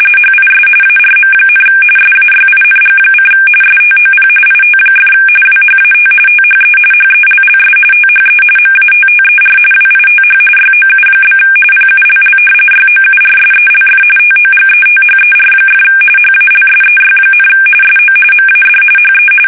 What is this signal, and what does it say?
RTTY, about 45.45 baud. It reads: HIS ARM, SAYING: STOP IT! WHAT THE HELLS THE MATTER WITH YOU? PULL YOURSELF TOGETHER! HALSEY, SO MAD HE
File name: fsk850 / battleshipnj